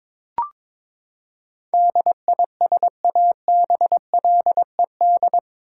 Morse code, heard as a keyed tone, about 21 wpm, 700 Hz. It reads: DISABLED
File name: practice / 22wpm / Eight Letter Words - 22wpm